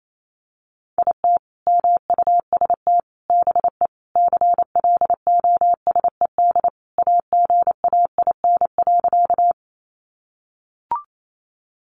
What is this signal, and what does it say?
Morse code, heard as a keyed tone, about 28 wpm, 700 Hz. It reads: IT MUST BE CLOSED AGAIN.
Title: Sentences from Top 1400 Words 28wpm